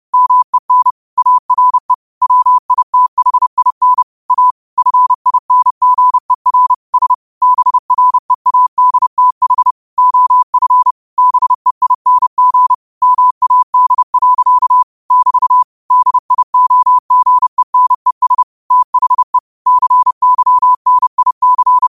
Morse code, heard as a keyed tone, about 31 wpm, 1000 Hz. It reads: MEN ARE WITHIN A FINGER S BREADTH OF BEING MAD. <BT> DIOGENES THE CYNIC
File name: qotdfiles / 30wpm